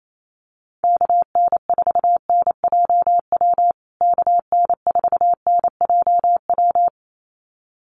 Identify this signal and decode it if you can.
Morse code, heard as a keyed tone, about 28 wpm, 700 Hz. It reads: KN4NJW KN4NJW
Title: USA Call Signs Repeated 2x 28wpm